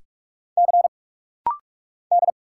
Morse code, heard as a keyed tone, about 46 wpm, 700 Hz. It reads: C D
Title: Single Letters - Rapid-Fire 45wpm